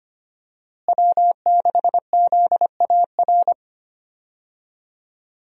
Morse code, heard as a keyed tone, about 25 wpm, 700 Hz. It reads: W6ZAR
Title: USA Call Signs - 5 Characters - Repeated 2x - 25wpm